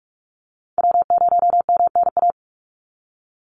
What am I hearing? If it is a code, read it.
Morse code, heard as a keyed tone, about 46 wpm, 690 Hz. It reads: W0MNA
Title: NA SOTA Chaser Callsigns 45wpm